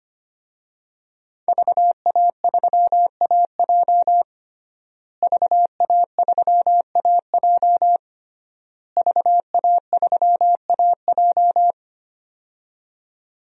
Morse code, heard as a keyed tone, about 25 wpm, 700 Hz. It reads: VA3AJ VA3AJ VA3AJ
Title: DXCC-100 Call Signs - 5 Characters - Repeated 3x - 25wpm